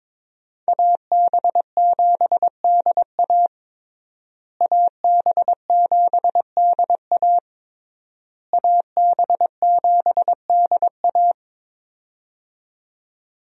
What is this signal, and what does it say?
Morse code, heard as a keyed tone, about 22 wpm, 700 Hz. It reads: AB7DA AB7DA AB7DA
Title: USA Call Signs - 5 Characters - Repeated 3x - 22wpm